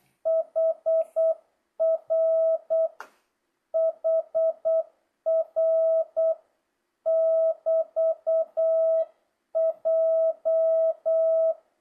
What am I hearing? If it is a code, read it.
Morse code, heard as a keyed tone, about 8 wpm, 650 Hz. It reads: HR HR <BT>J
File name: jh8cbh / 2025cw